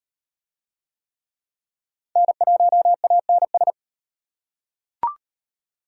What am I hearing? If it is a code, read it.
Morse code, heard as a keyed tone, about 38 wpm, 700 Hz. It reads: N1ANS